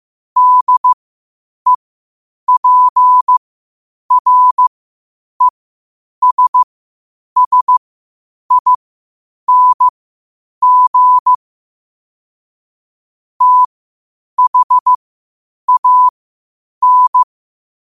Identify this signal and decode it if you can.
Morse code, heard as a keyed tone, about 15 wpm, spaced as for 8 wpm, 1000 Hz. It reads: DEPRESSING THAN
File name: qotdfiles / 8wpm